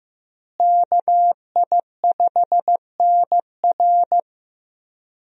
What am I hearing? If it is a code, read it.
Morse code, heard as a keyed tone, about 16 wpm, 700 Hz. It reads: KI5NR
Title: USA Call Signs - 5 Characters - Repeated 2x - 15wpm